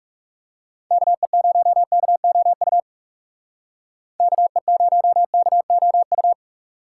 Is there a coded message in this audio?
Yes: Morse code from 0.9 to 6.3 s, heard as a keyed tone, about 38 wpm, 700 Hz. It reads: KE0KOU KE0KOU